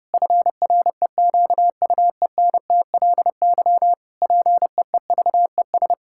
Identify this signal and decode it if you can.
Morse code, heard as a keyed tone, about 30 wpm, 700 Hz. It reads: FREQUENTLY PEEVES
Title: Pangram Challenge v2 - 30wpm